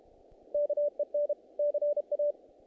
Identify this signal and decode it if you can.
Morse code, heard as a keyed tone, about 32 wpm, 590 Hz. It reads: KEN CA
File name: contests / naqp-cw-k5tr-op-k5pi- -2000